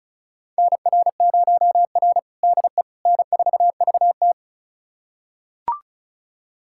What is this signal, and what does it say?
Morse code, heard as a keyed tone, about 36 wpm, 700 Hz. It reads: NR0R DE N4VT
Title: USA Callsign DE Callsign - 4 Characters - 35wpm